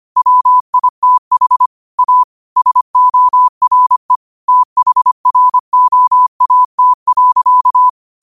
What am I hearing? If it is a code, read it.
Morse code, heard as a keyed tone, about 25 wpm, 1000 Hz. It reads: WITH A SORE THROAT.